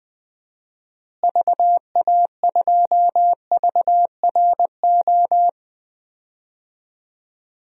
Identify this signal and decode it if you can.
Morse code, heard as a keyed tone, about 20 wpm, 700 Hz. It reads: VA2VRO
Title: DXCC-100 Call Signs - 6 Characters - Repeated 2x - 20wpm